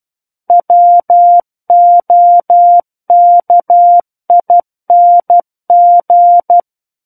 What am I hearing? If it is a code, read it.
Morse code, heard as a keyed tone, about 12 wpm, 700 Hz. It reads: WOKING
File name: Book One / 03-Chapter 3 - 12 WPM